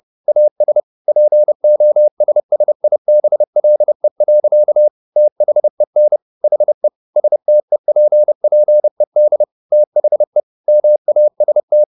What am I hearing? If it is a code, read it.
Morse code, heard as a keyed tone, about 30 wpm, 600 Hz. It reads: AS POSSIBLE. THEN HE STEPPED THE MAST